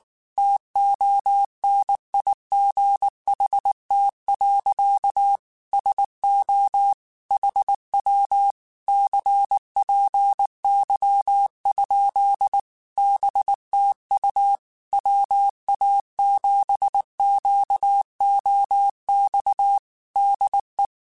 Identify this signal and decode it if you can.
Morse code, heard as a keyed tone, about 19 wpm, 800 Hz. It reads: TONIGHT. SO HW CPY? BTU WA7QOX DE
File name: QSOs / QSOs-mp3s / WA7QOX 1@18